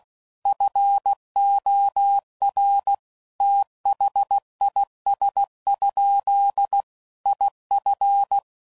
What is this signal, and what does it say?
Morse code, heard as a keyed tone, about 16 wpm, 800 Hz. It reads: FOR THIS? IF
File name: CW Code Groups @15WPM / Technical Subjects / OR Gate